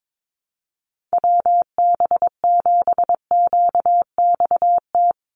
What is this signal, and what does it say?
Morse code, heard as a keyed tone, about 22 wpm, 700 Hz. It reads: WB7QXT
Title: USA Call Signs 22wpm